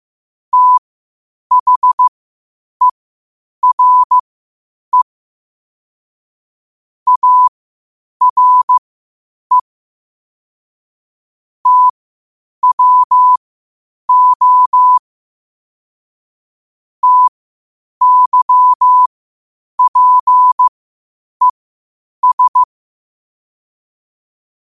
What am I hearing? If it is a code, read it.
Morse code, heard as a keyed tone, about 15 wpm, spaced as for 8 wpm, 1000 Hz. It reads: THERE ARE TWO TYPES
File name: qotdfiles / 8wpm